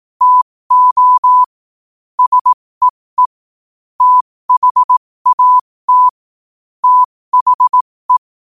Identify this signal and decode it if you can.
Morse code, heard as a keyed tone, about 18 wpm, spaced as for 15 wpm, 1000 Hz. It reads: TO SEE THAT THE
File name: qotdfiles / 15wpm